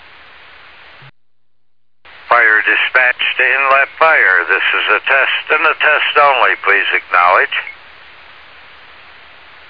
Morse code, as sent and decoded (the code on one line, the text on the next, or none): none